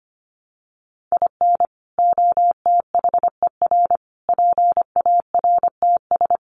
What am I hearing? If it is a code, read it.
Morse code, heard as a keyed tone, about 25 wpm, 700 Hz. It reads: IN OTHER PARTS
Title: Sets of 5 Words 25wpm